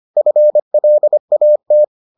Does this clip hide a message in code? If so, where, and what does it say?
Yes: Morse code from 0.2 to 1.8 s, heard as a keyed tone, about 25 wpm, 600 Hz. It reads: FLAT